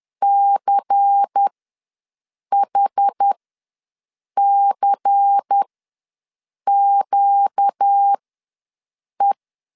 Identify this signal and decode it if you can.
Morse code, heard as a keyed tone, about 10 wpm, spaced as for 6 wpm, 800 Hz. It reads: CHCQE
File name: LEZIONI / 5 LBVC / 5 LBVC ws 10wpm 3733